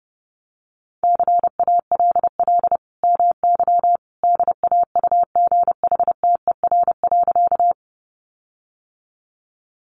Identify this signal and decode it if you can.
Morse code, heard as a keyed tone, about 30 wpm, 700 Hz. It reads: CALL MY DAUGHTER.
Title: Sentences from Top 1500 Words 30wpm